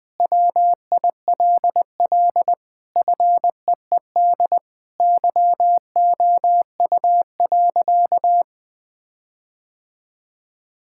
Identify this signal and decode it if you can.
Morse code, heard as a keyed tone, about 21 wpm, 700 Hz. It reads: WILL FEED YOU.
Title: Sentences from Top 1600 Words 20wpm